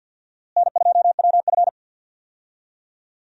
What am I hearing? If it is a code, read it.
Morse code, heard as a keyed tone, about 50 wpm, 700 Hz. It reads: N2WF
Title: USA Call Signs - 4 Characters - Repeated 2x - 50wpm